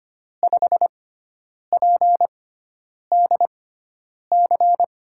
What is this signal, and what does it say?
Morse code, heard as a keyed tone, about 25 wpm, spaced as for 9 wpm, 700 Hz. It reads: 5PDC
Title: USA Call Signs - 6 Characters - 6x Character Spacing - 25wpm